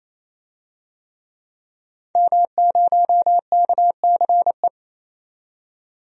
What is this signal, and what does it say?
Morse code, heard as a keyed tone, about 28 wpm, 700 Hz. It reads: M0KCE
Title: DXCC-100 Call Signs - 5 Characters - 28wpm